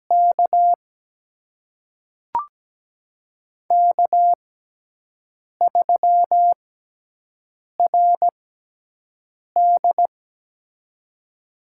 Morse code, heard as a keyed tone, about 17 wpm, spaced as for 6 wpm, 700 Hz. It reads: K K3RD
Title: USA Call Signs - 6x Character Spacing - 17wpm